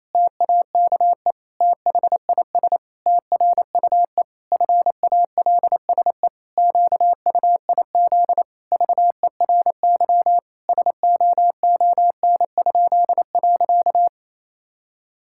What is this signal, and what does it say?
Morse code, heard as a keyed tone, about 28 wpm, 700 Hz. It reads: TAKE THIS TRUE FALSE QUIZ VERY SOON?.
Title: Pangram Challenge v2 - 28wpm